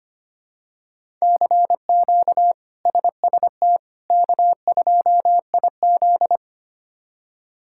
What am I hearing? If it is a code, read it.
Morse code, heard as a keyed tone, about 25 wpm, 700 Hz. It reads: CQ SST K2IZ
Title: K1USN SST QSO Practice - 25wpm